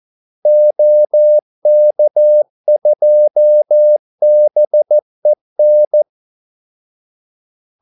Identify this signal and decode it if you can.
Morse code, heard as a keyed tone, about 14 wpm, 600 Hz. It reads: OK2BEN
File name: Callsigns-14wpm